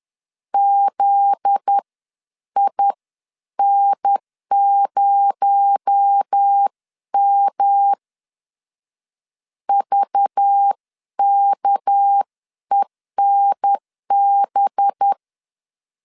Morse code, heard as a keyed tone, about 10 wpm, spaced as for 9 wpm, 800 Hz. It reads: ZIN0M VKENB
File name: LEZIONI / 7 GF9Z / 7 GF9Z 10wpm 2611